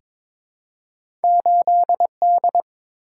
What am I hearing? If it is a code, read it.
Morse code, heard as a keyed tone, about 22 wpm, 700 Hz. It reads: 8D
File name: practice / 22wpm / Two Letter-Number Combinations - 22wpm